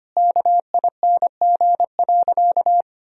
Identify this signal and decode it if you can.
Morse code, heard as a keyed tone, about 25 wpm, 700 Hz. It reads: KING.
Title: Sentences from Top 400 Words and Abbreviations 25wpm